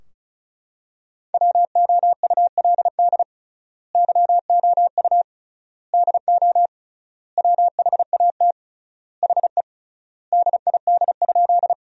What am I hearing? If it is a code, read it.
Morse code, heard as a keyed tone, about 36 wpm, 700 Hz. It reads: WOULD YOU DO WHAT HE DID?
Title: 3x Word-Spacing - Sentences from Top 100 Words - 35wpm